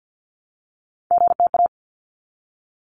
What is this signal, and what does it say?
Morse code, heard as a keyed tone, about 48 wpm, 700 Hz. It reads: GTA
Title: ARRL Sections 50wpm